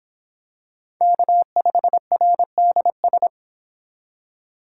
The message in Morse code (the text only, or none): K5RDS